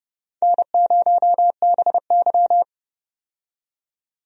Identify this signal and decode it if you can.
Morse code, heard as a keyed tone, about 30 wpm, 700 Hz. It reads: N0BY
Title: USA Call Signs - 4 Characters - 30wpm